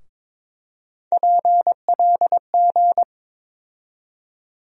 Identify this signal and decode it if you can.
Morse code, heard as a keyed tone, about 22 wpm, 700 Hz. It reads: PLG
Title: USA Call Sign Postfixes - 3 Letters 22wpm